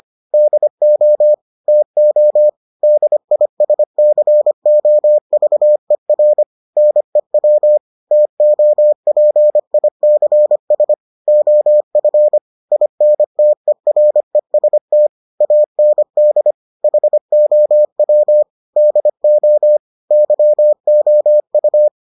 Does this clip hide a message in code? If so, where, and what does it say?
Yes: Morse code from 0.3 to 21.9 s, heard as a keyed tone, about 25 wpm, 600 Hz. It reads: DO TO DISCOVER NEW TOPICS OF INTEREST AND HOW DO YOU